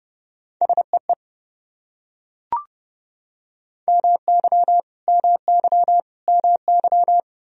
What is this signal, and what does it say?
Morse code, heard as a keyed tone, about 31 wpm, 700 Hz. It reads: SEE MY MY MY
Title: Top 100 Words Repeated 30wpm